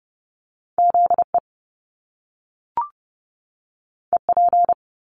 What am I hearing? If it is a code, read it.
Morse code, heard as a keyed tone, about 30 wpm, 700 Hz. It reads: ZE EP
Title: Top Two Letter Combinations in Words 30wpm